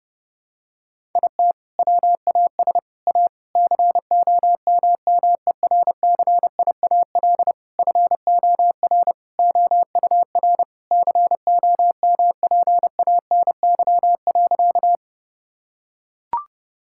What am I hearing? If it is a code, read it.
Morse code, heard as a keyed tone, about 30 wpm, 700 Hz. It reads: IT WAS A COMMERCIAL FOR OUR COMPANY.
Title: Sentences from Top 1000 Words - Review Part 5 - 30wpm